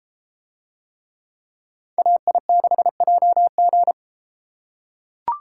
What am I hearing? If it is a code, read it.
Morse code, heard as a keyed tone, about 33 wpm, 700 Hz. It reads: AI6JG